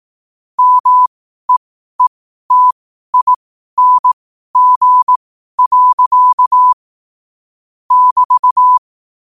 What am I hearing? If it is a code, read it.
Morse code, heard as a keyed tone, about 18 wpm, spaced as for 12 wpm, 1000 Hz. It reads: MEETING. <BT>
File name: qotdfiles / 12wpm